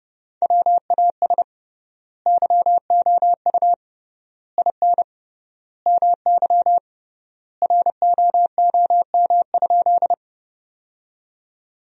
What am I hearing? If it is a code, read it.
Morse code, heard as a keyed tone, about 30 wpm, 700 Hz. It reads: WAS YOU IN MY ROOM?
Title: 3x Word-Spacing - Sentences from Top 300 Words - 30wpm